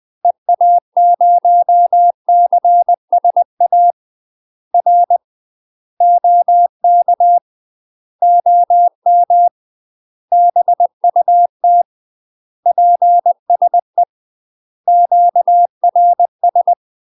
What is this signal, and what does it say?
Morse code, heard as a keyed tone, about 20 wpm, 700 Hz. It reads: EA0CSA R OK OM BUT PSE QRS
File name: UK MASTER QSOs TRAINING/20WPM